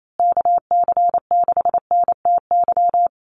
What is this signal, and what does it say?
Morse code, heard as a keyed tone, about 28 wpm, 700 Hz. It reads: KC6NTY
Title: USA Call Signs Repeated 2x 28wpm